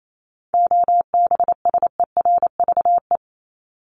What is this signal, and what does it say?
Morse code, heard as a keyed tone, about 28 wpm, 700 Hz. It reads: OBSERVE
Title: ED Suffix - Top 100 Words with Base Words 28wpm